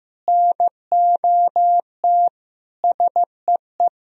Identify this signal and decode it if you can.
Morse code, heard as a keyed tone, about 15 wpm, 700 Hz. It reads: NOT SEE
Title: Sentences from Top 1000 Words - Review Part 2 - 15wpm